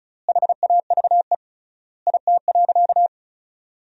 Morse code, heard as a keyed tone, about 36 wpm, 700 Hz. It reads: HAVE IT.
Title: 3x Word-Spacing - Sentences from Top 100 Words - 35wpm